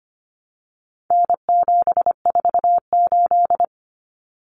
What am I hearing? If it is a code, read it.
Morse code, heard as a keyed tone, about 25 wpm, 700 Hz. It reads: N748
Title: Common Cut Numbers - 4-Digit 25wpm